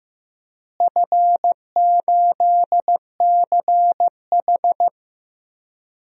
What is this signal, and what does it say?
Morse code, heard as a keyed tone, about 15 wpm, 700 Hz. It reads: F8CH
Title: DXCC-100 Call Signs - 4 Characters - Repeated 3x - 15wpm